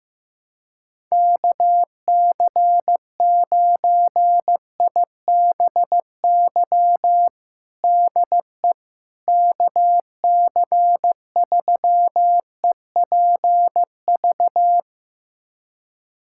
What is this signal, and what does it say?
Morse code, heard as a keyed tone, about 15 wpm, 700 Hz. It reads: KC9IBY DE KC3EPV
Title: USA Callsign DE Callsign - 6 Characters - 15wpm